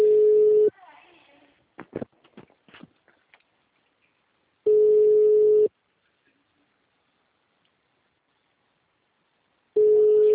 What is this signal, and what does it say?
Ringback tone (Europe, 425 Hz), with speech.
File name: out- -2013- - - .16791